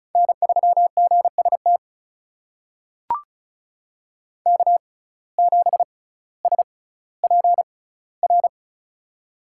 Morse code, heard as a keyed tone, about 36 wpm, spaced as for 18 wpm, 700 Hz. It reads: N3GST K7SPR
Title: USA Call Signs - 5 Characters - 6x Character Spacing - 35wpm